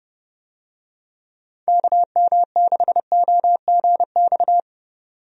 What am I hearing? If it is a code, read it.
Morse code, heard as a keyed tone, about 31 wpm, 700 Hz. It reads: KM6OGX
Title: USA Call Signs - 6 Characters - 30wpm